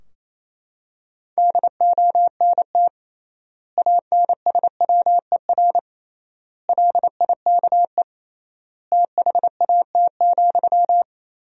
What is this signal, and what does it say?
Morse code, heard as a keyed tone, about 28 wpm, 700 Hz. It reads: DONT ANSWER LIKE THAT,
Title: 3x Word-Spacing - Sentences from Top 400 Words - 28wpm